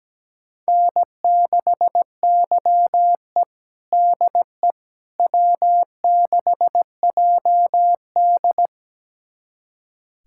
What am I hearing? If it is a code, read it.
Morse code, heard as a keyed tone, about 17 wpm, 700 Hz. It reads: N6YE DE W6JD
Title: USA Callsign DE Callsign - 4 Characters - 17wpm